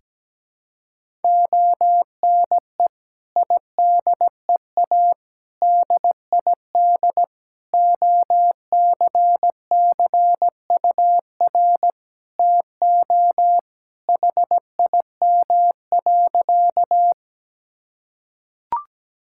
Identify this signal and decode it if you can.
Morse code, heard as a keyed tone, about 17 wpm, 700 Hz. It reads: ONE IDEA DID OCCUR TO HIM.
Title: Sentences from Top 1000 Words - Review Part 6 - 17wpm